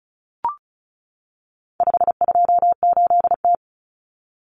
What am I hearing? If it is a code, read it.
Morse code, heard as a keyed tone, about 34 wpm, 700 Hz. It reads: 528T
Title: Common Cut Numbers - 4-Digit 35wpm